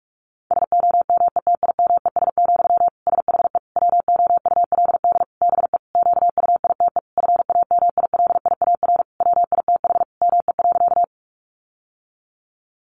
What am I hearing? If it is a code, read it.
Morse code, heard as a keyed tone, about 44 wpm, 700 Hz. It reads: SOMETIMES, SHE WOULD BE QUITE FAMILIAR WITH ME.
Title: Sentences from Top 1000 Words 45wpm